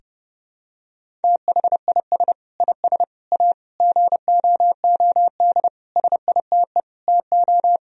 Morse code, heard as a keyed tone, about 30 wpm, 700 Hz. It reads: THIS IS A GOOD SITE TO
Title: Sentences from Top 1000 Words - Review Part 4 - 30wpm